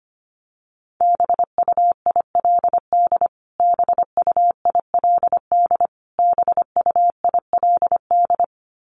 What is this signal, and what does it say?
Morse code, heard as a keyed tone, about 25 wpm, 700 Hz. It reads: BUILD BUILD BUILD
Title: Top 200 Words Repeated 25wpm